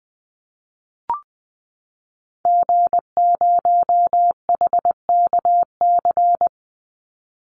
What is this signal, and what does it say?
Morse code, heard as a keyed tone, about 19 wpm, 700 Hz. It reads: G0HKC
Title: Call Signs Repeated 2x - 20wpm